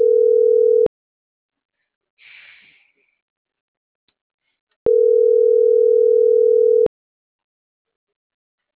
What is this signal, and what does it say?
Ringback tone (North America), 440 + 480 Hz.